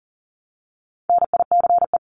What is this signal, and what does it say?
Morse code, heard as a keyed tone, about 40 wpm, 700 Hz. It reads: NICE
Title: Top 751-1000 Words 40wpm